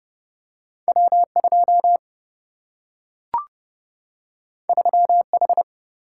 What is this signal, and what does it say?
Morse code, heard as a keyed tone, about 30 wpm, 700 Hz. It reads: W 2 3 H
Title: Single-Letter Single-Number 30wpm